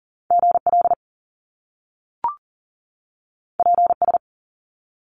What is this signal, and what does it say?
Morse code, heard as a keyed tone, about 42 wpm, 700 Hz. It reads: GL PS
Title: Top Two Letter Combinations in Words 40wpm